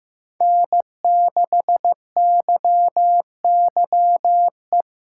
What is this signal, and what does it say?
Morse code, heard as a keyed tone, about 15 wpm, 700 Hz. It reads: N6YYE
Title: USA Callsign DE Callsign - 5 Characters - 15wpm